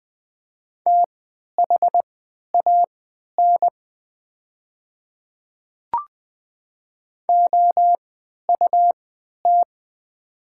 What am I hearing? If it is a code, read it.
Morse code, heard as a keyed tone, about 20 wpm, spaced as for 11 wpm, 700 Hz. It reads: THAN OUT
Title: 3x Character Spacing - Top 100 Words - 20wpm